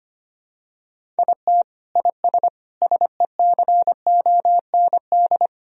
Morse code, heard as a keyed tone, about 25 wpm, 700 Hz. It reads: IT IS SECOND
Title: Sentences from Top 1000 Words - Review Part 1 - 25wpm